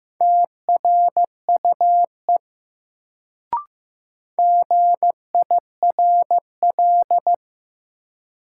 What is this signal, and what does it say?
Morse code, heard as a keyed tone, about 15 wpm, 700 Hz. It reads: TRUE GIRL
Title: Sentences from Top 400 Words - New Word Drill - 15wpm